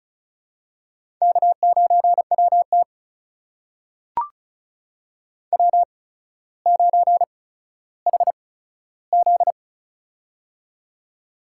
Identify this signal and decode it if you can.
Morse code, heard as a keyed tone, about 36 wpm, spaced as for 15 wpm, 700 Hz. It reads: K9WT W9HZ
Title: USA Call Signs - 4 Characters - 8x Character Spacing - 35wpm